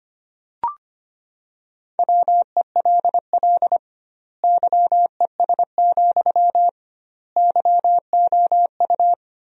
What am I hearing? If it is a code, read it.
Morse code, heard as a keyed tone, about 25 wpm, 700 Hz. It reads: WELL YES, YOU
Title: 2x Word-Spacing - Sentences from Top 500 Words - 25wpm